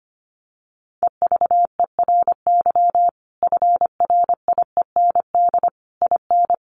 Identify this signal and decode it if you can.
Morse code, heard as a keyed tone, about 25 wpm, 700 Hz. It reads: EVERY FRIEND IN
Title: Sentences from Top 1400 Words 25wpm